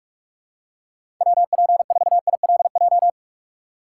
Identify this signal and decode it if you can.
Morse code, heard as a keyed tone, about 46 wpm, 700 Hz. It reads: WP4ILJ
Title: USA Call Signs - 6 Characters - 2x Character Spacing - 45wpm